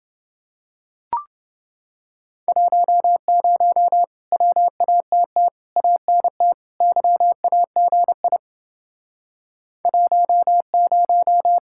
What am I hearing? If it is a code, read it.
Morse code, heard as a keyed tone, about 30 wpm, 700 Hz. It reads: 10 WATT ANT YAGI 10